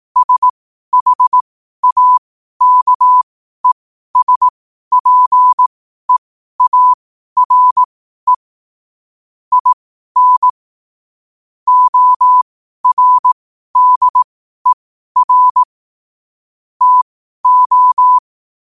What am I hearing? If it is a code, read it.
Morse code, heard as a keyed tone, about 18 wpm, spaced as for 12 wpm, 1000 Hz. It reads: SHAKESPEARE IN ORDER TO
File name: qotdfiles / 12wpm